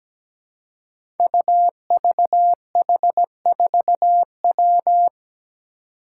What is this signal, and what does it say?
Morse code, heard as a keyed tone, about 17 wpm, 700 Hz. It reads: UVH4W
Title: Five Challenging Letter-Number Combinations 17wpm